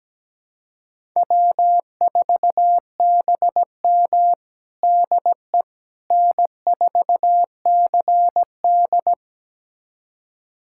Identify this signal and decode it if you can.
Morse code, heard as a keyed tone, about 17 wpm, 700 Hz. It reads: W4BM DE N4CD
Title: USA Callsign DE Callsign - 4 Characters - 17wpm